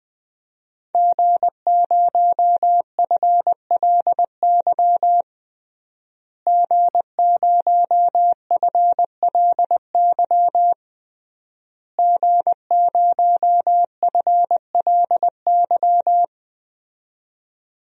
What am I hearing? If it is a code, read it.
Morse code, heard as a keyed tone, about 20 wpm, 700 Hz. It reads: G0FLY G0FLY G0FLY
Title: DXCC-100 Call Signs - 5 Characters - Repeated 3x - 20wpm